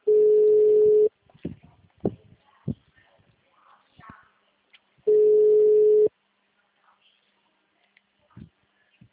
Ringback tone (Europe, 425 Hz).
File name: out- -2012- - - .2276